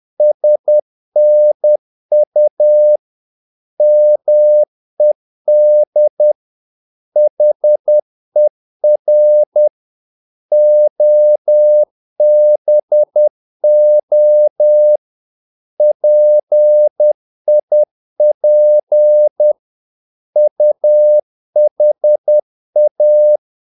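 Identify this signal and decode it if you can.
Morse code, heard as a keyed tone, about 10 wpm, 600 Hz. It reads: SNU MED HER OBO PIP UHA
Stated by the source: Morse practice of short Danish words at 10 wpm, from CW med Gnister.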